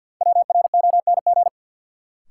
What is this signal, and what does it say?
Morse code, heard as a keyed tone, about 51 wpm, 700 Hz. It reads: WRONG
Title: Sets of 5 Words Encore - New Word Drill - 50wpm